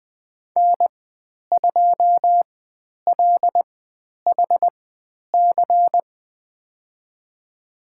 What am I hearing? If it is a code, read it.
Morse code, heard as a keyed tone, about 20 wpm, spaced as for 10 wpm, 700 Hz. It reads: N2LHC
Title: 095 Lesson 16g - Letter c - Call Signs 20wpm